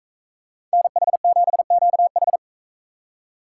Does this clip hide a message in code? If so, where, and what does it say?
Yes: Morse code from 0.7 to 2.4 s, heard as a keyed tone, about 42 wpm, 700 Hz. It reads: NH7QH